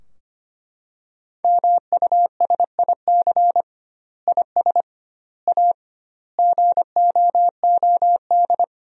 Morse code, heard as a keyed tone, about 25 wpm, 700 Hz. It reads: MUSIC IS A GOOD
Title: 2x Word-Spacing - Sentences from Top 300 Words - 25wpm